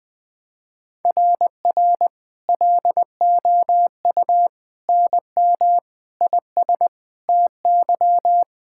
Morse code, heard as a keyed tone, about 20 wpm, 700 Hz. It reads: RR LOU NM IS TY
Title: SOTA QSO Practice 20wpm